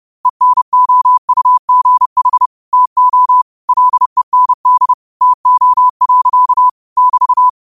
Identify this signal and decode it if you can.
Morse code, heard as a keyed tone, about 30 wpm, 1000 Hz. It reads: ENOUGH TO LEND TO. <BT>
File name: qotdfiles / 30wpm